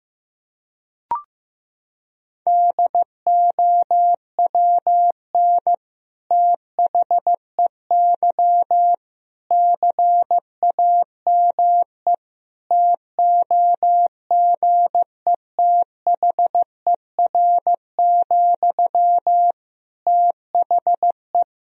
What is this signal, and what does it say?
Morse code, heard as a keyed tone, about 15 wpm, 700 Hz. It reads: DOWN THEY CAME TOGETHER, THE
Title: Sentences from Top 1000 Words - Review Part 2 - 15wpm